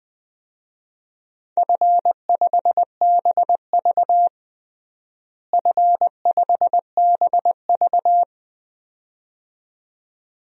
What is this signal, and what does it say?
Morse code, heard as a keyed tone, about 20 wpm, 700 Hz. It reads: F5BV F5BV
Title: DXCC-100 Call Signs - 4 Characters - Repeated 2x - 20wpm